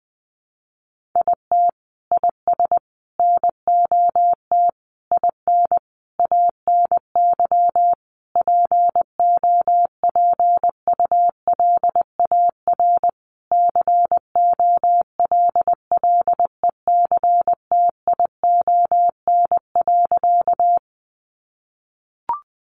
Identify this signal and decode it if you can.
Morse code, heard as a keyed tone, about 20 wpm, 700 Hz. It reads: IT IS NOT IN ANY POPULAR COLLECTION.
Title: Sentences from Top 1200 Words 20wpm